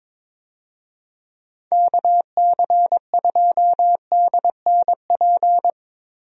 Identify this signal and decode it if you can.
Morse code, heard as a keyed tone, about 22 wpm, 700 Hz. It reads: KC2DNP
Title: USA Call Signs - 6 Characters - 22wpm